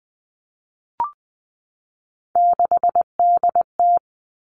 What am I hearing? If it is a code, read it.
Morse code, heard as a keyed tone, about 20 wpm, 700 Hz. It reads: 6DT
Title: Three Challenging Letter-Number Combinations 20wpm